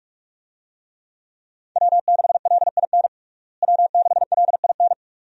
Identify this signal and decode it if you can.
Morse code, heard as a keyed tone, about 46 wpm, 700 Hz. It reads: W6LIN W6LIN
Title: USA Call Signs - 5 Characters - Repeated 2x - 45wpm